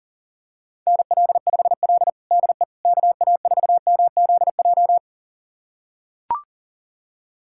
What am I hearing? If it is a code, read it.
Morse code, heard as a keyed tone, about 37 wpm, 700 Hz. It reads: NL5L DE KA4MZJ
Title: USA Callsign DE Callsign 40wpm